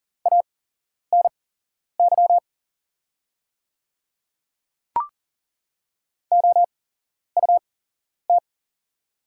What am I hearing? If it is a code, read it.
Morse code, heard as a keyed tone, about 40 wpm, spaced as for 11 wpm, 700 Hz. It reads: ANY OUT
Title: 8x Character Spacing - Top 200 Words - 40wpm